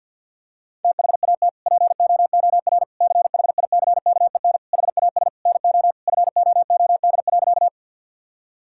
Morse code, heard as a keyed tone, about 50 wpm, 700 Hz. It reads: THAT POOR CHICKEN HAS NO FOOD.
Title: Sentences from Top 1000 Words - Review Part 5 - 50wpm